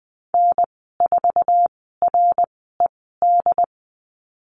Morse code, heard as a keyed tone, about 20 wpm, spaced as for 15 wpm, 700 Hz. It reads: N4RED
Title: 069 Lesson 12g - Letter d - Call Signs 20wpm